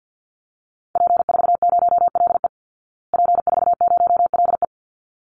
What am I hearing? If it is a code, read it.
Morse code, heard as a keyed tone, about 50 wpm, 700 Hz. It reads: P40LE P40LE
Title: Call Signs Repeated 3x - 50wpm